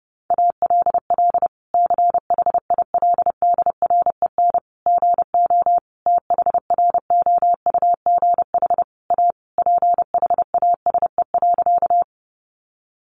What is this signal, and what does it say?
Morse code, heard as a keyed tone, about 30 wpm, 700 Hz. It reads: ALL CHILDREN GO THROUGH A PHASE.
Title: Sentences from Top 1200 Words 30wpm